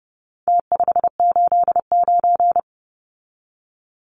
Morse code, heard as a keyed tone, about 30 wpm, 700 Hz. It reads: T589
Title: Numbers - Contest Style - 4-Digit 30wpm